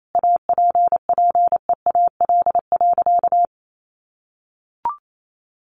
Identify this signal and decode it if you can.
Morse code, heard as a keyed tone, about 28 wpm, 700 Hz. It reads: APPEAL.
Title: Sentences from Top 1300 Words 28wpm